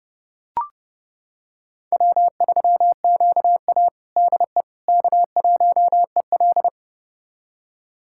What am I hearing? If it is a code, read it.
Morse code, heard as a keyed tone, about 30 wpm, 700 Hz. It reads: W3QA DE K1EL
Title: USA Callsign DE Callsign - 4 Characters - 30wpm